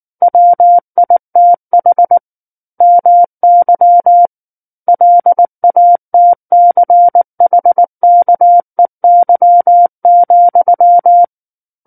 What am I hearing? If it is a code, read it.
Morse code, heard as a keyed tone, about 19 wpm, 700 Hz. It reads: WITH MY LATCHKEY,
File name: Book One / 10-Chapter 10 - 19 WPM